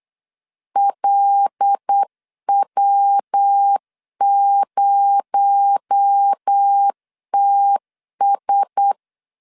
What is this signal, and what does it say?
Morse code, heard as a keyed tone, about 8 wpm, 800 Hz. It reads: LW0TS
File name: LEZIONI / 6 PKX0 / 6 PKX0 8wpm 3341